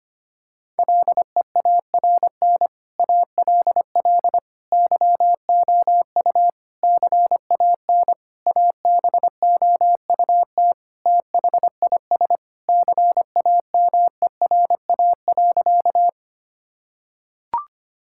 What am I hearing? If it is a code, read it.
Morse code, heard as a keyed tone, about 25 wpm, 700 Hz. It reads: LEARN ALL YOU CAN ABOUT THIS CAMERA.
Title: Sentences from Top 1000 Words - Review Part 5 - 25wpm